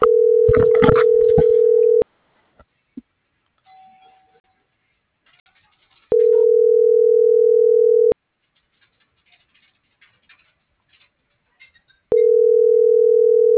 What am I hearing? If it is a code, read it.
Ringback tone (North America), 440 + 480 Hz.